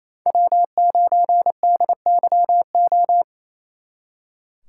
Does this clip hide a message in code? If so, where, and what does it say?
Yes: Morse code from 0.3 to 3.2 s, heard as a keyed tone, about 36 wpm, 700 Hz. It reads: W9DYO